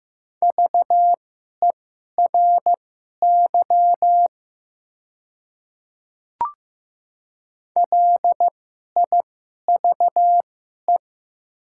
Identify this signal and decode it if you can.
Morse code, heard as a keyed tone, about 15 wpm, spaced as for 11 wpm, 700 Hz. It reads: VERY LIVE
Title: 2x Character Spacing - Top 200 Words - 15wpm